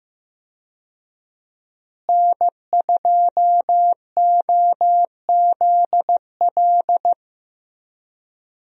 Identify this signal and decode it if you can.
Morse code, heard as a keyed tone, about 15 wpm, 700 Hz. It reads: N2OZL
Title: USA Call Signs - 5 Characters - Repeated 3x - 15wpm